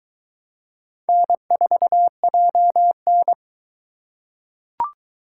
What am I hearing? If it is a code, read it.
Morse code, heard as a keyed tone, about 22 wpm, 700 Hz. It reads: N4JN